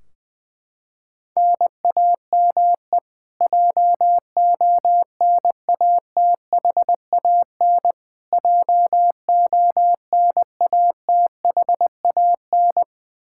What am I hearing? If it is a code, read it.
Morse code, heard as a keyed tone, about 20 wpm, 700 Hz. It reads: NAME JONATHAN JONATHAN
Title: Practice QSO - Exchange 1 - Beginning - 20wpm